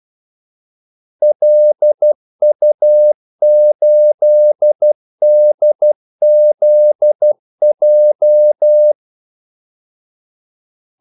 Morse code, heard as a keyed tone, about 12 wpm, 600 Hz. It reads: LU8DZJ
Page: 2 ekstra ordmellemrum mellem kaldesignalerne